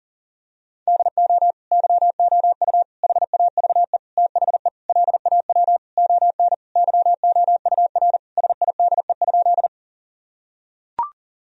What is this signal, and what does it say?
Morse code, heard as a keyed tone, about 37 wpm, 700 Hz. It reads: DO YOU HAVE THE LAW ON YOUR SIDE?
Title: Sentences from Top 1000 Words - Review Part 3 - 40wpm